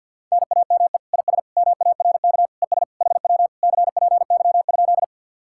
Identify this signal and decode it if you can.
Morse code, heard as a keyed tone, about 62 wpm, 700 Hz. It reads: NAME IS MARK ES HW CPY?
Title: Speed-Racing Common QSO Phrases - 50wpm